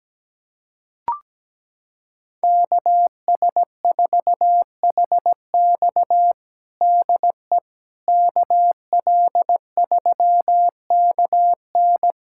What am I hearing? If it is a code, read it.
Morse code, heard as a keyed tone, about 17 wpm, 700 Hz. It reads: KS4HX DE KL3KN
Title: USA Callsign DE Callsign - 5 Characters - 17wpm